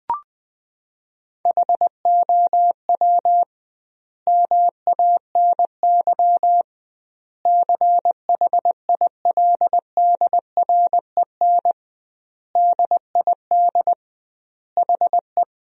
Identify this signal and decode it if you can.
Morse code, heard as a keyed tone, about 21 wpm, 700 Hz. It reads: HOW MANY CHILDREN DID HE
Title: 2x Word-Spacing - Sentences from Top 400 Words - 20wpm